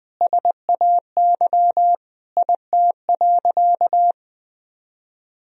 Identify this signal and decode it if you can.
Morse code, heard as a keyed tone, about 20 wpm, 700 Hz. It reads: SAY IT.
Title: Sentences from Top 400 Words and Abbreviations 20wpm